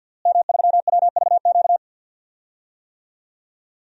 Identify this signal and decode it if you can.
Morse code, heard as a keyed tone, about 49 wpm, 700 Hz. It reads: M3WVQ
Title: DXCC-100 Call Signs - 5 Characters - Repeated 2x - 50wpm